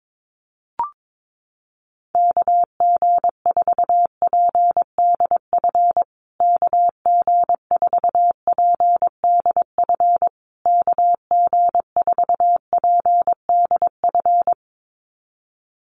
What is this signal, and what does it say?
Morse code, heard as a keyed tone, about 22 wpm, 690 Hz. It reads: KG4PDF KG4PDF KG4PDF
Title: USA Call Signs Repeated 3x 22wpm